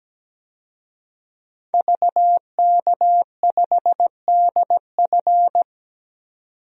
Morse code, heard as a keyed tone, about 17 wpm, 700 Hz. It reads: VK5DF
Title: DXCC-100 Call Signs - 5 Characters - 17wpm